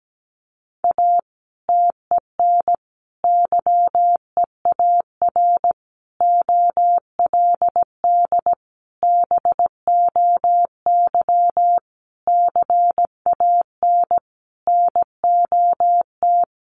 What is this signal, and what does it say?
Morse code, heard as a keyed tone, about 17 wpm, 700 Hz. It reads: A TEN YEAR OLD BOY CAN NOT
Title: Sentences from Top 1300 Words 17wpm